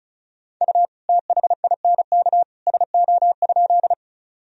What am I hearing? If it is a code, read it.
Morse code, heard as a keyed tone, about 34 wpm, 700 Hz. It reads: U THINK SO?
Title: Sentences from Top 200 Words and Abbreviations 35wpm